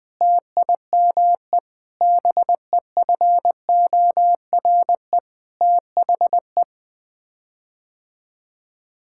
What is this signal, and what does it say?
Morse code, heard as a keyed tone, about 20 wpm, 700 Hz. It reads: TIME BEFORE THE
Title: Sets of 9 Words - 20wpm